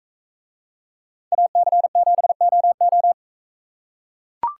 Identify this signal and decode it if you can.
Morse code, heard as a keyed tone, about 42 wpm, 700 Hz. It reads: AC7OO